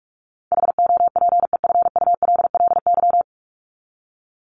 Morse code, heard as a keyed tone, about 44 wpm, 700 Hz. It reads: HOPEFULLY
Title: Top 1251-1500 Words 45wpm